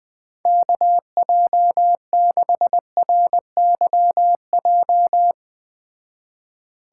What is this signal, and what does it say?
Morse code, heard as a keyed tone, about 19 wpm, 700 Hz. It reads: KJ6RYJ
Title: Speed Racing - USA 2x3 Call Signs 17wpm